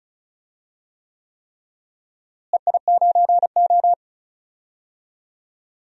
Morse code, heard as a keyed tone, about 36 wpm, 700 Hz. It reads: EI9O
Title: DXCC-100 Call Signs - 4 Characters - Repeated 3x - 35wpm